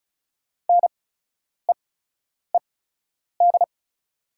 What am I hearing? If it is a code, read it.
Morse code, heard as a keyed tone, about 34 wpm, spaced as for 10 wpm, 690 Hz. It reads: NEED
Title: 8x Character Spacing - Top 500 Words - 35wpm